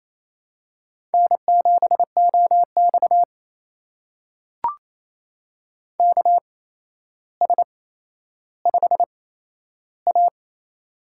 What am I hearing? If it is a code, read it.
Morse code, heard as a keyed tone, about 28 wpm, spaced as for 12 wpm, 700 Hz. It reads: N7OX KS5A
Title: USA Call Signs - 4 Characters - 8x Character Spacing - 28wpm